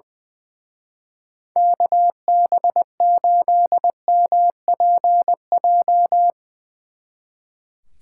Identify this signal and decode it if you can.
Morse code, heard as a keyed tone, about 22 wpm, 700 Hz. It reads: KB8MPJ
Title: Speed Racing - USA 2x3 Call Signs 20wpm